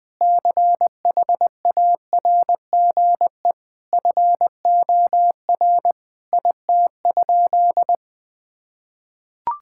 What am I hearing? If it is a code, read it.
Morse code, heard as a keyed tone, about 20 wpm, 700 Hz. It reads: CHARGE FOR IT?
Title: Sentences from Top 1000 Words - Review Part 6 - 20wpm